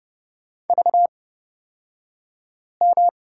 Morse code, heard as a keyed tone, about 30 wpm, spaced as for 5 wpm, 700 Hz. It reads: VM
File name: Lesson 38d - Letter q - Two Characters 30wpm